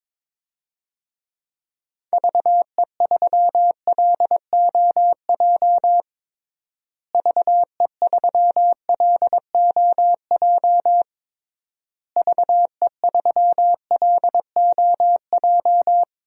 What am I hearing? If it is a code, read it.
Morse code, heard as a keyed tone, about 21 wpm, 700 Hz. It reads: VE3LOJ VE3LOJ VE3LOJ
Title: DXCC-100 Call Signs - 6 Characters - Repeated 3x - 22wpm